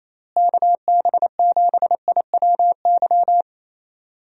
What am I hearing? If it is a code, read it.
Morse code, heard as a keyed tone, about 27 wpm, 700 Hz. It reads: KB7IWY
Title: USA Call Signs - 6 Characters - Repeated 3x - 28wpm